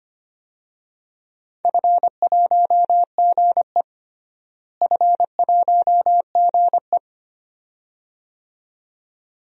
Morse code, heard as a keyed tone, about 25 wpm, 700 Hz. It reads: F1GE F1GE
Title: DXCC-100 Call Signs - 4 Characters - Repeated 2x - 25wpm